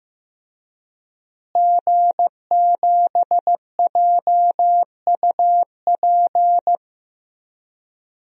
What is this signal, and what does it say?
Morse code, heard as a keyed tone, about 15 wpm, 700 Hz. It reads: G7JUP
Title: DXCC-100 Call Signs - 5 Characters - Repeated 2x - 15wpm